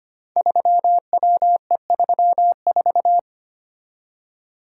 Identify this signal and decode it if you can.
Morse code, heard as a keyed tone, about 25 wpm, 700 Hz. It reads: 3WE34
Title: Code Groups - 5 Letter:Numbers - 25wpm